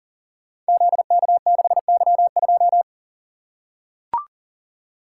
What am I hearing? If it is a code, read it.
Morse code, heard as a keyed tone, about 40 wpm, 700 Hz. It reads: ZK6Y2
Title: Code Groups - 5 Letter:Numbers - 40wpm